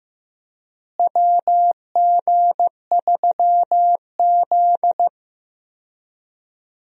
Morse code, heard as a keyed tone, about 16 wpm, 700 Hz. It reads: WG3Z
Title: USA Call Signs - 4 Characters - Repeated 2x - 15wpm